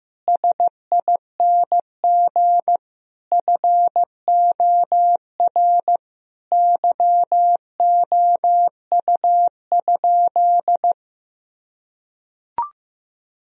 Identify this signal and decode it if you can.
Morse code, heard as a keyed tone, about 15 wpm, 700 Hz. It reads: SING FOR YOU?
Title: Sentences from Top 1000 Words - Review Part 2 - 15wpm